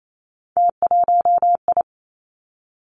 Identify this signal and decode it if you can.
Morse code, heard as a keyed tone, about 28 wpm, 700 Hz. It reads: T1I
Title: Three Challenging Letter-Number Combinations 28wpm